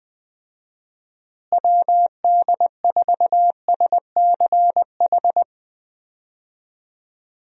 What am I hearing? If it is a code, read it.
Morse code, heard as a keyed tone, about 21 wpm, 700 Hz. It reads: WD4SCH
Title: USA Call Signs - 6 Characters - Repeated 2x - 20wpm